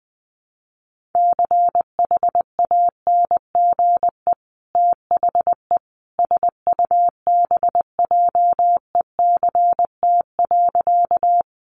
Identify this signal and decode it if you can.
Morse code, heard as a keyed tone, about 20 wpm, 700 Hz. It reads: CHANGE THE SUBJECT.
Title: Sentences from Top 800 Words 20wpm